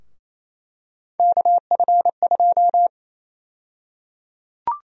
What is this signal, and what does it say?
Morse code, heard as a keyed tone, about 28 wpm, 700 Hz. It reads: KF2
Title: USA Call Sign Prefixes - 28wpm